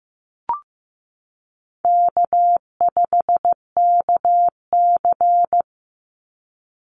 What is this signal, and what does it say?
Morse code, heard as a keyed tone, about 15 wpm, 700 Hz. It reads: K5KC
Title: Easy Call Signs - 15wpm